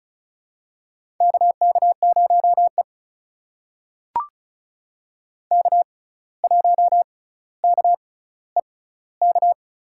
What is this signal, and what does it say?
Morse code, heard as a keyed tone, about 36 wpm, spaced as for 17 wpm, 700 Hz. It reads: KK0E K1KEK
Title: USA Call Signs - 6x Character Spacing - 35wpm